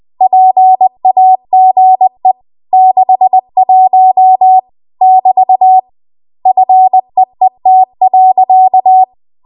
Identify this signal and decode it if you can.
Morse code, heard as a keyed tone, about 20 wpm, 750 Hz. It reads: PAGE 61 <BT> FEET.
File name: Archive / 20 WPM / 20WPM